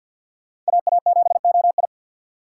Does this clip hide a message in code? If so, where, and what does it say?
Yes: Morse code from 0.7 to 1.9 s, heard as a keyed tone, about 64 wpm, 700 Hz. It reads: AA7OI